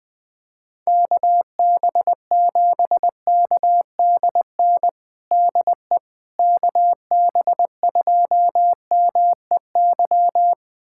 Morse code, heard as a keyed tone, about 20 wpm, 700 Hz. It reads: KB7KDN DE KB2MEY
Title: USA Callsign DE Callsign - 6 Characters - 20wpm